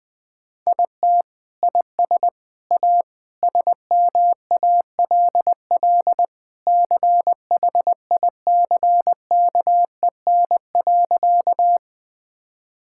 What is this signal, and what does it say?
Morse code, heard as a keyed tone, about 20 wpm, 700 Hz. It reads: IT IS A SMALL CHICKEN.
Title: Sentences from Top 1000 Words - Review Part 5 - 20wpm